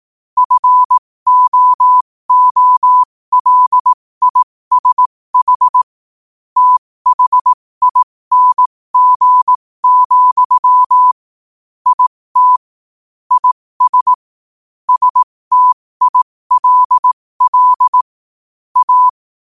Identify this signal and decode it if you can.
Morse code, heard as a keyed tone, about 18 wpm, spaced as for 15 wpm, 1000 Hz. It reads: FOOLISH THING, IT IS STILL A
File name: qotdfiles / 15wpm